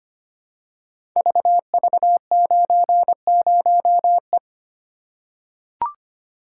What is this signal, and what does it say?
Morse code, heard as a keyed tone, about 25 wpm, 700 Hz. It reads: VV90E
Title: Code Groups - 5 Letter:Numbers - 25wpm